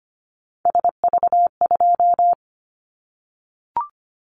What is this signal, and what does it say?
Morse code, heard as a keyed tone, about 25 wpm, 700 Hz. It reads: SV2
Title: Callsign Prefixes for Top 50 DXCC - 25wpm